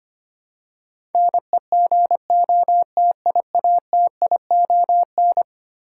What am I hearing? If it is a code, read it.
Morse code, heard as a keyed tone, about 25 wpm, 700 Hz. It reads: NEGOTIATION
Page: morse-code-ninja-practice-25wpm